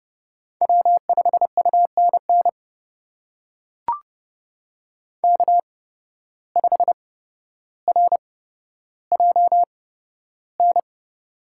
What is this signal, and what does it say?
Morse code, heard as a keyed tone, about 29 wpm, spaced as for 13 wpm, 700 Hz. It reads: W5UNN K5RJN
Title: USA Call Signs - 5 Characters - 8x Character Spacing - 30wpm